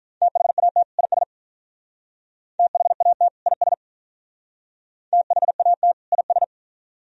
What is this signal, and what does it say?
Morse code, heard as a keyed tone, about 45 wpm, 700 Hz. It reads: THAT IS THAT IS THAT IS